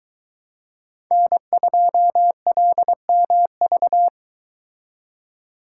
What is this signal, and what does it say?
Morse code, heard as a keyed tone, about 24 wpm, 700 Hz. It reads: N2LMV